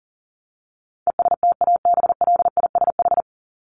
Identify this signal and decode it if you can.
Morse code, heard as a keyed tone, about 40 wpm, 700 Hz. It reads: ESTABLISH
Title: ED Suffix - Top 100 Words with Base Words 40wpm